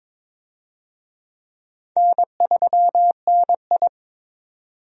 Morse code, heard as a keyed tone, about 22 wpm, 700 Hz. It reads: N3NI
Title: USA Call Signs - 4 Characters - 22wpm